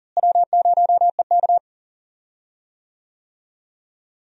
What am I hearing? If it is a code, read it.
Morse code, heard as a keyed tone, about 38 wpm, 700 Hz. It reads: W0EK
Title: USA Call Signs - 4 Characters - Repeated 3x - 40wpm